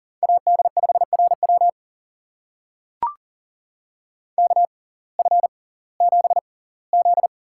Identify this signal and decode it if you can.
Morse code, heard as a keyed tone, about 39 wpm, spaced as for 22 wpm, 700 Hz. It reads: AD5RW KF7Z
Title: USA Call Signs - 5 Characters - 6x Character Spacing - 40wpm